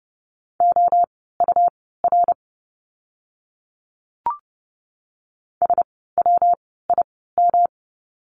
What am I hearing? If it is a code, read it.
Morse code, heard as a keyed tone, about 30 wpm, spaced as for 17 wpm, 700 Hz. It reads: OUR SWIM
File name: course / 30:10wpm / 115 Lesson 19f - Letter w - Words 30wpm